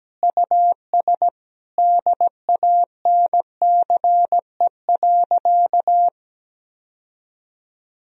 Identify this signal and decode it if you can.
Morse code, heard as a keyed tone, about 17 wpm, 700 Hz. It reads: US DANCE.
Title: Sentences from Top 1000 Words - Review Part 4 - 17wpm